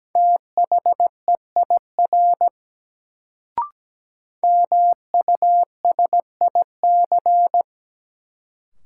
Morse code, heard as a keyed tone, about 17 wpm, 700 Hz. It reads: THEIR MUSIC
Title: Sentences from Top 300 Words - New Word Drill - 17wpm